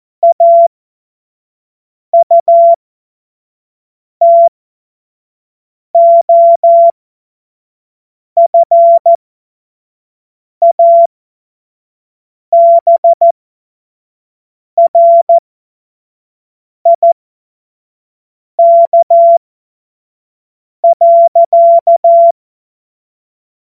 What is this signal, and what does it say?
Morse code, heard as a keyed tone, about 14 wpm, spaced as for 5 wpm, 680 Hz. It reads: AUTOFABRIK.
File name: Morsekurs / Witze / Witz 22. November 2022 25 F70